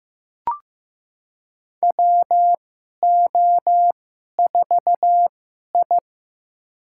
Morse code, heard as a keyed tone, about 15 wpm, spaced as for 11 wpm, 700 Hz. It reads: WO4I
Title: USA Call Signs - 2x Character Spacing - 15wpm